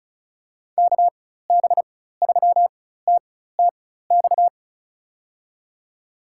Morse code, heard as a keyed tone, about 36 wpm, spaced as for 16 wpm, 700 Hz. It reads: KB3TTX
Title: USA Call Signs - 6 Characters - 4x Character Spacing - 35wpm